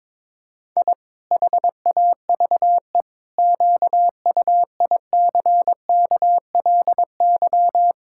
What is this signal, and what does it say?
Morse code, heard as a keyed tone, about 22 wpm, 700 Hz. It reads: I HAVE QUICKLY
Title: Pangram Challenge v2 - 22wpm